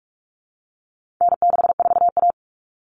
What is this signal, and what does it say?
Morse code, heard as a keyed tone, about 44 wpm, 700 Hz. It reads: N64A
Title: Common Cut Numbers - 4-Digit 45wpm